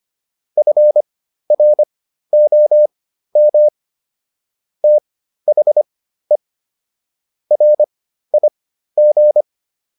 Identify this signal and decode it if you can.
Morse code, heard as a keyed tone, about 25 wpm, spaced as for 13 wpm, 600 Hz. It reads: FROM THE RIG